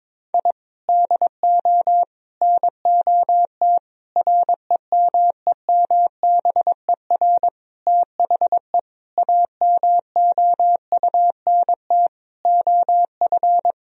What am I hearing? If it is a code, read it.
Morse code, heard as a keyed tone, about 22 wpm, 700 Hz. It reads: I DO NOT REMEMBER THE AMOUNT OF
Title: Sentences from Top 1000 Words - Review Part 4 - 22wpm